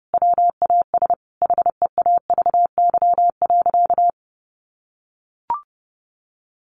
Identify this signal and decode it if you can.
Morse code, heard as a keyed tone, about 30 wpm, 700 Hz. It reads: WAS HEAVY.
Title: Sentences from Top 1200 Words 30wpm